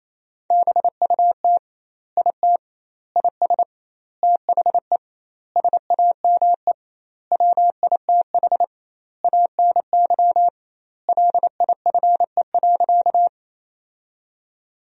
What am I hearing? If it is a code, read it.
Morse code, heard as a keyed tone, about 28 wpm, 700 Hz. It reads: BUT IT IS THE SAME WITH ANY LIFE.
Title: 2x Word-Spacing - Sentences from Top 300 Words - 28wpm